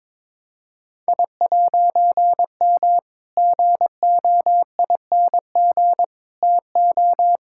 Morse code, heard as a keyed tone, about 22 wpm, 700 Hz. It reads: I'M GOING TO
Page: morse-code-ninja-practice-22wpm